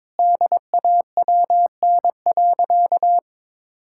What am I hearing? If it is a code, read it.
Morse code, heard as a keyed tone, about 22 wpm, 700 Hz. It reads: DAWN.
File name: Difficult Sentences with Hints - 22wpm